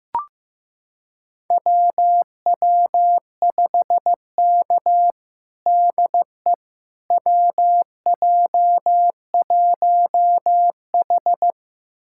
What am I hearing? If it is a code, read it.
Morse code, heard as a keyed tone, about 15 wpm, 700 Hz. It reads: WW5K DE WJ1H
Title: USA Callsign DE Callsign - 4 Characters - 15wpm